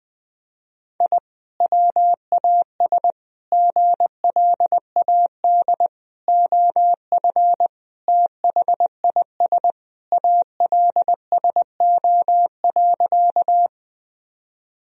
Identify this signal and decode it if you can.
Morse code, heard as a keyed tone, about 19 wpm, 700 Hz. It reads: I WAS GLAD OF THIS ALSO.
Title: Sentences from Top 1000 Words - Review Part 1 - 20wpm